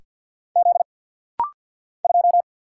Morse code, heard as a keyed tone, about 50 wpm, 690 Hz. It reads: Z 2
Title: Single Letter-Number Alt - Rapid-Fire 50wpm